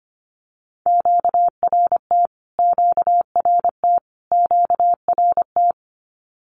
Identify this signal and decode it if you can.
Morse code, heard as a keyed tone, about 25 wpm, 700 Hz. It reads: QRT QRT QRT
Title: Q-Codes Repeated 3x 25wpm